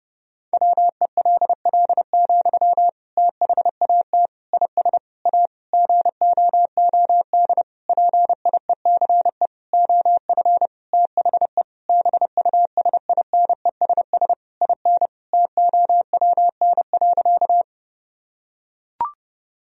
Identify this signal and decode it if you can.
Morse code, heard as a keyed tone, about 30 wpm, 700 Hz. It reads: WELL, THAT IS A GOOD PIECE OF THE BUSINESS IN TOWN.
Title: Sentences from Top 1000 Words - Review Part 2 - 30wpm